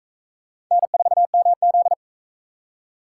Morse code, heard as a keyed tone, about 41 wpm, 700 Hz. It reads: N4MZ